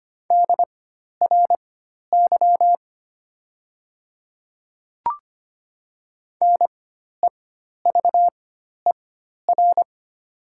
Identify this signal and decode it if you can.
Morse code, heard as a keyed tone, about 25 wpm, spaced as for 12 wpm, 700 Hz. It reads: DRY NEVER
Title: 4x Character Spacing - Top 500 Words - 25wpm